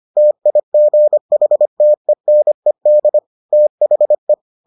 Morse code, heard as a keyed tone, about 25 wpm, 600 Hz. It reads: TIGHTENED THE